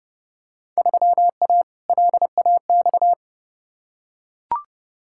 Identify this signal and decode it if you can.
Morse code, heard as a keyed tone, about 30 wpm, 700 Hz. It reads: 3A LAX
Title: Field Day Exchanges - 30wpm